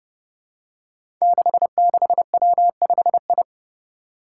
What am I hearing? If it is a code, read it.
Morse code, heard as a keyed tone, about 28 wpm, 700 Hz. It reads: 66W5I
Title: Five Challenging Letter-Number Combinations 30wpm